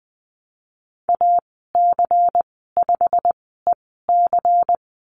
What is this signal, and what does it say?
Morse code, heard as a keyed tone, about 20 wpm, spaced as for 15 wpm, 700 Hz. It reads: AC5EC
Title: 122 Lesson 20g - Number 3 - Call Signs 20wpm